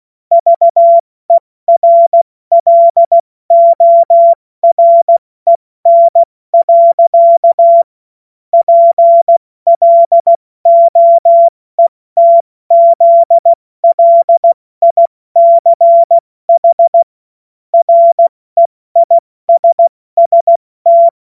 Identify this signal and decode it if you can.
Morse code, heard as a keyed tone, about 16 wpm, 680 Hz. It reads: VERLOREN. PLOETZLICH REISST
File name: Morsekurs / Witze / Witz 16. August 70 F80